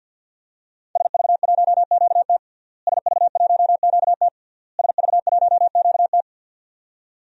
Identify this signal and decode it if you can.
Morse code, heard as a keyed tone, about 47 wpm, 700 Hz. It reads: SV1QT SV1QT SV1QT
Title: DXCC-100 Call Signs - 5 Characters - Repeated 3x - 50wpm